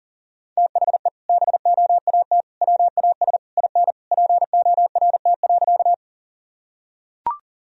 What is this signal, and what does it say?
Morse code, heard as a keyed tone, about 40 wpm, 700 Hz. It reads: THE BOAT WAS IN PORT.
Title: Sentences from Top 1000 Words - Review Part 1 - 40wpm